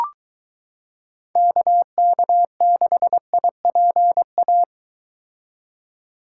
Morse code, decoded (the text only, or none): KK6IPA